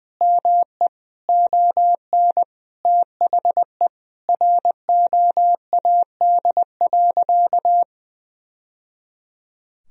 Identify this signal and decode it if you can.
Morse code, heard as a keyed tone, about 20 wpm, 700 Hz. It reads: ME ON THE ROAD.
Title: Sentences from Top 1000 Words - Review Part 2 - 20wpm